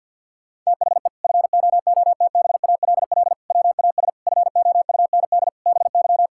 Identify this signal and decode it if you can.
Morse code, heard as a keyed tone, about 51 wpm, 700 Hz. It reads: THE FOOTBALL WAS FOUND BY
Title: Sentences from Top 1600 Words 50wpm